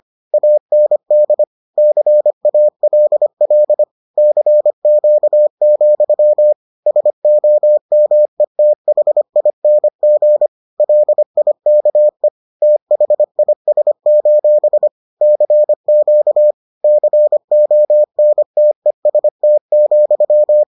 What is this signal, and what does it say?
Morse code, heard as a keyed tone, about 25 wpm, 600 Hz. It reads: AND CALL CQ, SOMETHING LIKE THIS: CQ CONTEST,